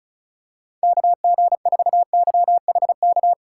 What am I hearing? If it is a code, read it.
Morse code, heard as a keyed tone, about 36 wpm, 700 Hz. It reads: KG4YHK
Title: USA Call Signs - 2x Character Spacing - 35wpm